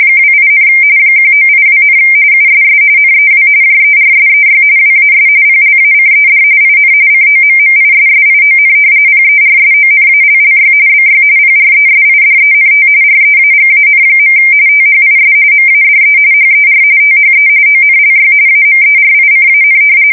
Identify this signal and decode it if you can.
RTTY, about 45.45 baud. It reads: 28, 1982, USS NEW JERSEY WAS PLACED IN COMMISSION PERSONALLY BY PRESIDENT REAGAN. THIS MADE HER THE ONLY AMERICAN
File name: fsk170 / battleshipnj